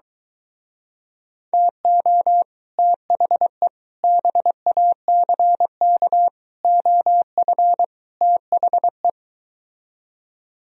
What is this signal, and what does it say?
Morse code, heard as a keyed tone, about 23 wpm, 700 Hz. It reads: TO THE BACK OF THE